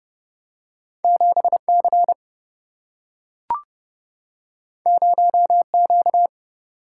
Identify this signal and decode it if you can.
Morse code, heard as a keyed tone, about 30 wpm, 700 Hz. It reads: 7 C 0 Q
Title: Single-Letter Single-Number 30wpm